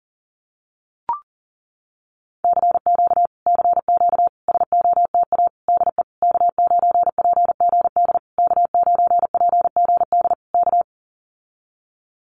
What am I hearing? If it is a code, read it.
Morse code, heard as a keyed tone, about 40 wpm, 700 Hz. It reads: CQ CQ SOTA DE K9PGD K9PGD K
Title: Common QSO Phrases Encore 40wpm